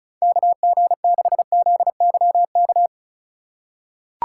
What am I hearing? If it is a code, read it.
Morse code, heard as a keyed tone, about 36 wpm, 700 Hz. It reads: KG6ZYK
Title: USA Call Signs - 4x Character Spacing - 35wpm